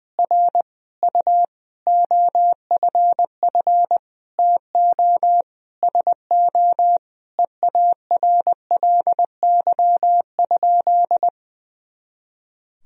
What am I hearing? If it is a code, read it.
Morse code, heard as a keyed tone, about 21 wpm, 700 Hz. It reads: R U OFF TO SO EARLY?
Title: Sentences from Top 400 Words and Abbreviations 20wpm